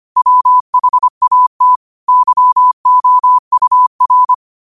Morse code, heard as a keyed tone, about 25 wpm, 1000 Hz. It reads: WHAT YOUR
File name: qotdfiles / 25wpm